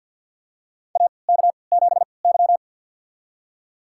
Morse code, heard as a keyed tone, about 47 wpm, spaced as for 28 wpm, 700 Hz. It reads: AK7Y
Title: USA Call Signs - 4 Characters - 3x Character Spacing - 50wpm